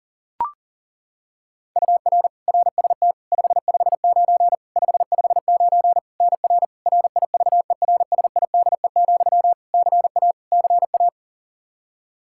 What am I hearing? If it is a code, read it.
Morse code, heard as a keyed tone, about 40 wpm, 700 Hz. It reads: UR RST 559 559 NR RIVERSIDE, CA CA
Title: Practice QSO - Exchange 1 - Beginning - 40wpm